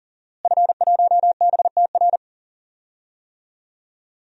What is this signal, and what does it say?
Morse code, heard as a keyed tone, about 40 wpm, 700 Hz. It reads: F1BTR
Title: DXCC-100 Call Signs - 5 Characters - Repeated 3x - 40wpm